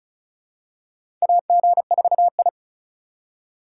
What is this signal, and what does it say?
Morse code, heard as a keyed tone, about 36 wpm, 700 Hz. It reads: AG4I
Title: USA Call Signs - 3x Character Spacing - 35wpm